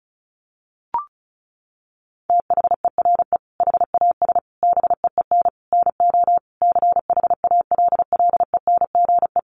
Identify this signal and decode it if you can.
Morse code, heard as a keyed tone, about 34 wpm, 700 Hz. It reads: THERE HAS BEEN NO CHALLENGE
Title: Sentences from Top 1500 Words 35wpm